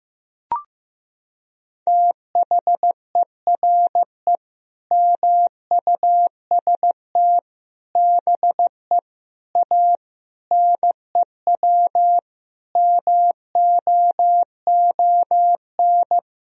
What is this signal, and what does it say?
Morse code, heard as a keyed tone, about 15 wpm, 700 Hz. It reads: THERE MUST BE A NEW MOON
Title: Sentences from Top 1000 Words - Review Part 3 - 15wpm